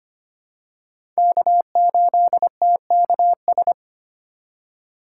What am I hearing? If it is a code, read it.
Morse code, heard as a keyed tone, about 25 wpm, 700 Hz. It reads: K8TKS
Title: USA Call Signs - 5 Characters - 4x Character Spacing - 25wpm